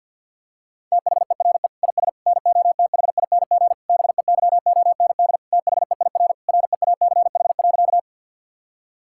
Morse code, heard as a keyed tone, about 50 wpm, 700 Hz. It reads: THERE IS NOTHING BEYOND THEIR REACH.
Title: Sentences from Top 1000 Words - Review Part 6 - 50wpm